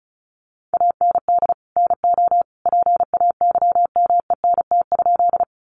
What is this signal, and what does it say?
Morse code, heard as a keyed tone, about 34 wpm, 700 Hz. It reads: AND NO PAYMENT?
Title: Sentences from Top 1200 Words 35wpm